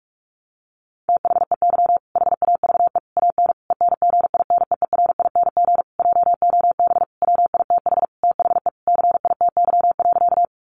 Morse code, heard as a keyed tone, about 44 wpm, 700 Hz. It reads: THEY HAVE AN ENGINEERING JOB WITH THE CITY.
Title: Sentences from Top 1500 Words 45wpm